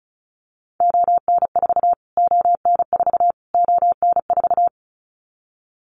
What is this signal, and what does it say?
Morse code, heard as a keyed tone, about 34 wpm, 700 Hz. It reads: ON4 ON4 ON4
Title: Callsign Prefixes for Top 25 DXCC - Repeated 3x - 35wpm